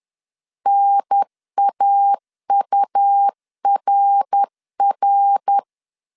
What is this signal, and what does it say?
Morse code, heard as a keyed tone, about 10 wpm, 800 Hz. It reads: NAURR
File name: CW / DET / 1 3 1500C S B 10wpm 3335